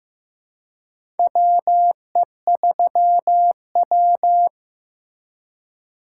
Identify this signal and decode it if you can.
Morse code, heard as a keyed tone, about 15 wpm, 700 Hz. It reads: WE3W
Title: USA Call Signs - 4 Characters - 2x Character Spacing - 15wpm